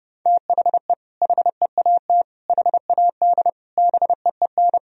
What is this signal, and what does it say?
Morse code, heard as a keyed tone, about 32 wpm, 700 Hz. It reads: THE HEAT HAD BEEN
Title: Sentences from Top 1000 Words - Review Part 3 - 30wpm